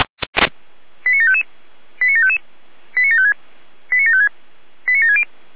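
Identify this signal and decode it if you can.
Five-tone selective call (ZVEI), heard as quick runs of five tones, tones of about 70 ms. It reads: 89755 89755 89756 89756 89760
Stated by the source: ZVEI five-tone selective call 89725 received on an SDR.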